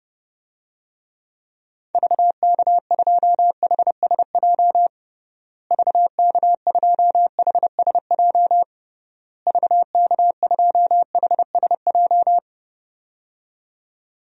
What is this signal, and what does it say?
Morse code, heard as a keyed tone, about 32 wpm, 700 Hz. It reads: VK2HSJ VK2HSJ VK2HSJ
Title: DXCC-100 Call Signs - 6 Characters - Repeated 3x - 30wpm